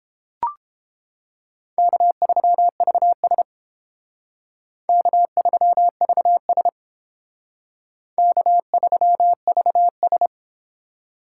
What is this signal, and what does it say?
Morse code, heard as a keyed tone, about 29 wpm, 700 Hz. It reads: K3VS K3VS K3VS